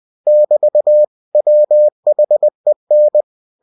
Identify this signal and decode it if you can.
Morse code, heard as a keyed tone, about 20 wpm, 600 Hz. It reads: <BT> WHEN